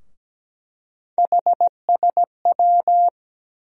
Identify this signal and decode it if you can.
Morse code, heard as a keyed tone, about 17 wpm, 700 Hz. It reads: HSW
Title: USA Call Sign Postfixes - 3 Letters 17wpm